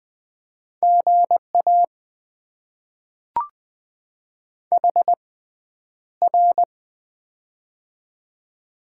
Morse code, heard as a keyed tone, about 20 wpm, spaced as for 10 wpm, 700 Hz. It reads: GA HR
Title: Top 90 QSO Elements - 6x Character Spacing - 20wpm